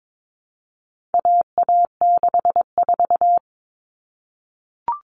Morse code, heard as a keyed tone, about 22 wpm, 700 Hz. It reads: AA64
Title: All Cut Numbers - 4-Digit 22wpm